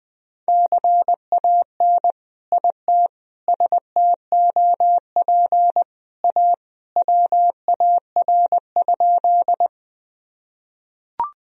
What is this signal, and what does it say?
Morse code, heard as a keyed tone, about 20 wpm, 700 Hz. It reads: CAN IT STOP A WAR?
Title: Sentences from Top 1000 Words - Review Part 2 - 20wpm